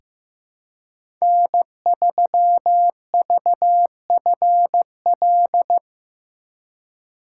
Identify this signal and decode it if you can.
Morse code, heard as a keyed tone, about 15 wpm, 700 Hz. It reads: N3VFL
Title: USA Call Signs - 5 Characters - 3x Character Spacing - 15wpm